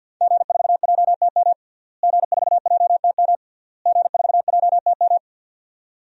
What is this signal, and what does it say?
Morse code, heard as a keyed tone, about 52 wpm, 700 Hz. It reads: G4JTM G4JTM G4JTM
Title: DXCC-100 Call Signs - 5 Characters - Repeated 3x - 50wpm